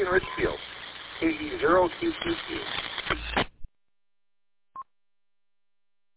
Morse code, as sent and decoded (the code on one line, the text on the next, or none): none